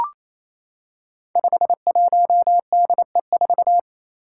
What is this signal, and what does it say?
Morse code, heard as a keyed tone, about 28 wpm, 700 Hz. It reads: 51DE4
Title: Five Challenging Letter-Number Combinations 28wpm